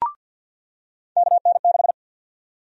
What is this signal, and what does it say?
Morse code, heard as a keyed tone, about 47 wpm, 700 Hz. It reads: KN6
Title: USA Call Sign Prefixes - 50wpm